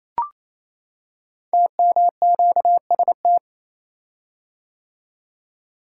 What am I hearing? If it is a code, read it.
Morse code, heard as a keyed tone, about 29 wpm, 700 Hz. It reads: TMQST
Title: Code Groups - 5 Letter:Numbers - 28wpm